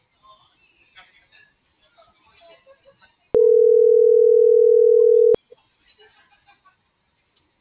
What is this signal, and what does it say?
Ringback tone (North America), 440 + 480 Hz.